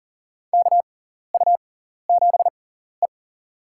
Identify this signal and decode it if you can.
Morse code, heard as a keyed tone, about 38 wpm, spaced as for 14 wpm, 700 Hz. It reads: KU7E
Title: USA Call Signs - 4 Characters - 6x Character Spacing - 40wpm